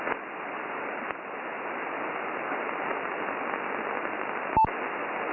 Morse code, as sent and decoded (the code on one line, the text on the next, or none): none